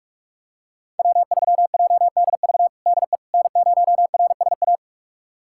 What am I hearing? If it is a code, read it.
Morse code, heard as a keyed tone, about 46 wpm, 700 Hz. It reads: W3JDV DE N0RSA
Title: USA Callsign DE Callsign - 5 Characters - 45wpm